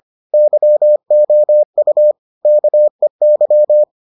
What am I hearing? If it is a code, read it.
Morse code, heard as a keyed tone, about 25 wpm, 600 Hz. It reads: YOU KEY